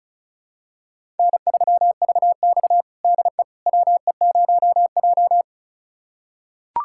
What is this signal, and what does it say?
Morse code, heard as a keyed tone, about 36 wpm, 700 Hz. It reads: N3VX DE WE0J
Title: USA Callsign DE Callsign - 4 Characters - 35wpm